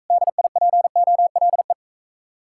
Morse code, heard as a keyed tone, about 45 wpm, 700 Hz. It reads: DIPOLE